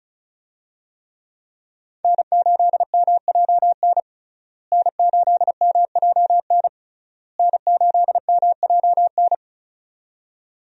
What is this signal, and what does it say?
Morse code, heard as a keyed tone, about 36 wpm, 700 Hz. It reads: N8MJN N8MJN N8MJN
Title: USA Call Signs - 5 Characters - Repeated 3x - 35wpm